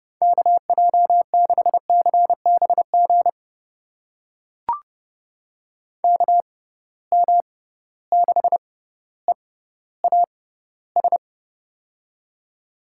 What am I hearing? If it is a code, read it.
Morse code, heard as a keyed tone, about 30 wpm, spaced as for 15 wpm, 700 Hz. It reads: KJ6CBG KM6EAS
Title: USA Call Signs - 6 Characters - 6x Character Spacing - 30wpm